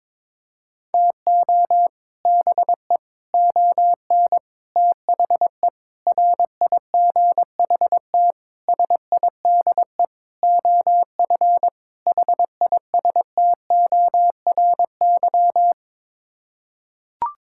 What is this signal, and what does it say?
Morse code, heard as a keyed tone, about 22 wpm, 700 Hz. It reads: TO BE ON THE RIGHT SIDE OF HISTORY
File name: practice / 22wpm / Sets of 8 Words - 22wpm